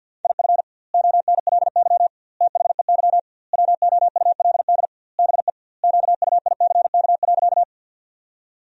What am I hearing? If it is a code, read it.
Morse code, heard as a keyed tone, about 51 wpm, 700 Hz. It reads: IF ONLY THEY WOULD BE QUICK.
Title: 2x Word-Spacing - Sentences from Top 500 Words - 50wpm